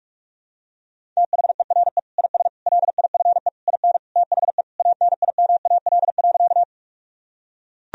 Morse code, heard as a keyed tone, about 43 wpm, 700 Hz. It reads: THERE IS LIFE IN THE ANIMAL.
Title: Sentences from Top 1000 Words - Review Part 1 - 45wpm